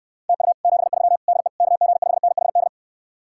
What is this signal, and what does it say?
Morse code, heard as a keyed tone, about 69 wpm, 700 Hz. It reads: TU 73 DE KG5NHD